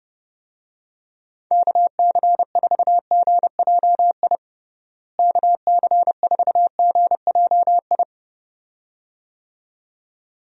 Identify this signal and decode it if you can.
Morse code, heard as a keyed tone, about 30 wpm, 700 Hz. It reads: KC4GJI KC4GJI
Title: USA Call Signs - 6 Characters - Repeated 2x - 30wpm